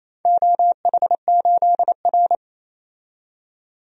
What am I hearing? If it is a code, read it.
Morse code, heard as a keyed tone, about 29 wpm, 700 Hz. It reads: OH8R
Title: DXCC-100 Call Signs - 4 Characters - 28wpm